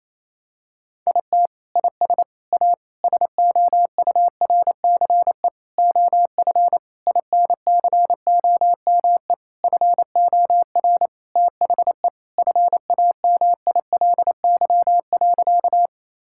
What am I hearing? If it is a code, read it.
Morse code, heard as a keyed tone, about 27 wpm, 700 Hz. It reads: IT IS A SOURCE OF INCOME FOR THE FAMILY.
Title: Sentences from Top 1000 Words - Review Part 5 - 28wpm